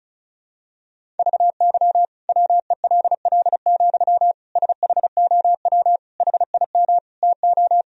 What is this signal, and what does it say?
Morse code, heard as a keyed tone, about 35 wpm, 700 Hz. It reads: VY WELL, SHOW HIM TO
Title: Sentences from Top 200 Words and Abbreviations 35wpm